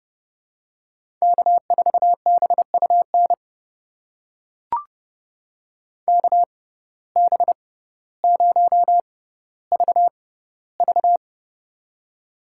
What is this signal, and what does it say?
Morse code, heard as a keyed tone, about 30 wpm, spaced as for 15 wpm, 700 Hz. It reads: K4BUN KB0VV
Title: USA Call Signs - 5 Characters - 6x Character Spacing - 30wpm